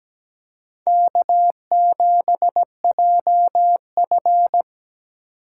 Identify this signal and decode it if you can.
Morse code, heard as a keyed tone, about 17 wpm, 700 Hz. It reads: K7JF
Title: USA Call Signs - 4 Characters - 2x Character Spacing - 17wpm